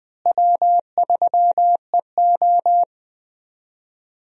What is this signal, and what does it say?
Morse code, heard as a keyed tone, about 20 wpm, 700 Hz. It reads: W3EO
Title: USA Call Signs - 4 Characters - Repeated 3x - 20wpm